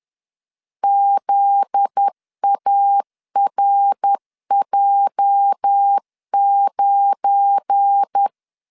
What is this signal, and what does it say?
Morse code, heard as a keyed tone, about 10 wpm, 800 Hz. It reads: ZARJ9
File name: CW / DET / 1 9 1500C S A 10wpm 4511